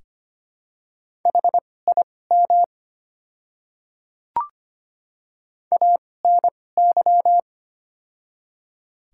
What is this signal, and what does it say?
Morse code, heard as a keyed tone, about 25 wpm, spaced as for 18 wpm, 700 Hz. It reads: HIM ANY
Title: 2x Character Spacing - Top 200 Words - 25wpm